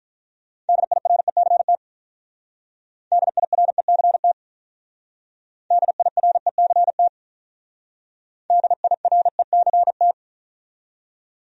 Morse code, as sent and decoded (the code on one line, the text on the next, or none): -.. .. .-. -.-. - / -.. .. .-. -.-. - / -.. .. .-. -.-. - / -.. .. .-. -.-. -
DIRECT DIRECT DIRECT DIRECT